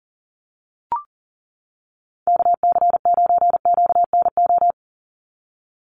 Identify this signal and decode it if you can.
Morse code, heard as a keyed tone, about 40 wpm, 700 Hz. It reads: KC9QNO
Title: USA Call Signs 40wpm